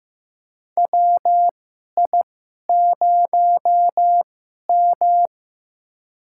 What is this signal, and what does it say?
Morse code, heard as a keyed tone, about 16 wpm, spaced as for 11 wpm, 700 Hz. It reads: WI0M
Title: USA Call Signs - 4 Characters - 2x Character Spacing - 15wpm